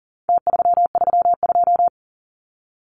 Morse code, heard as a keyed tone, about 40 wpm, 700 Hz. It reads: T332
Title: Numbers - Contest Style - 4-Digit 40wpm